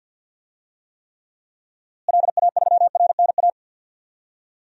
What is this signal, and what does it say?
Morse code, heard as a keyed tone, about 52 wpm, 700 Hz. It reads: LA3RNA
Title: DXCC-100 Call Signs - 6 Characters - Repeated 2x - 50wpm